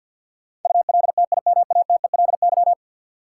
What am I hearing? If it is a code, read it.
Morse code, heard as a keyed tone, about 47 wpm, 700 Hz. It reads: ULTIMATELY
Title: Sentences from Top 1200 Words - New Word Drill - 50wpm